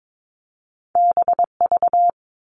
Morse code, heard as a keyed tone, about 22 wpm, 700 Hz. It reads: BV
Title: All Cut Numbers - 2-Digit 22wpm